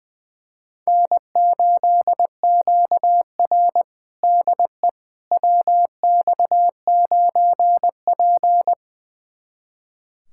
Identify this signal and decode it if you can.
Morse code, heard as a keyed tone, about 20 wpm, 700 Hz. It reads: N8QR DE WX9P
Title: USA Callsign DE Callsign - 4 Characters - 20wpm